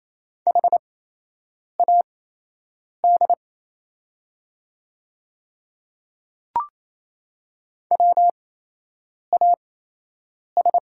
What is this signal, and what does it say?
Morse code, heard as a keyed tone, about 28 wpm, spaced as for 8 wpm, 700 Hz. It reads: HAD WAS
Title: 8x Character Spacing - Top 100 Words - 28wpm